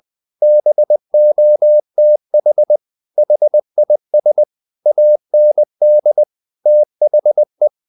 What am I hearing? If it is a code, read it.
Morse code, heard as a keyed tone, about 20 wpm, 600 Hz. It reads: BOTH HIS AND THE
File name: the old man and the sea part 2 20wpm